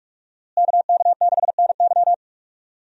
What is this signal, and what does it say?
Morse code, heard as a keyed tone, about 45 wpm, 700 Hz. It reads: KK6NY
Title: USA Call Signs - 5 Characters - Repeated 2x - 45wpm